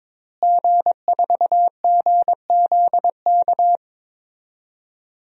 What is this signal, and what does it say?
Morse code, heard as a keyed tone, about 22 wpm, 700 Hz. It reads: G4GZK
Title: DXCC-100 Call Signs - 5 Characters - Repeated 2x - 22wpm